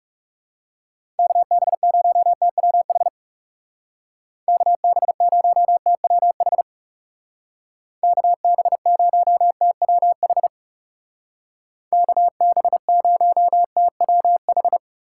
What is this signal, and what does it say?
Morse code, heard as a keyed tone, about 36 wpm, 700 Hz. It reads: KB0TWH KB0TWH KB0TWH KB0TWH
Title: Speed Racing - USA Call Signs 30wpm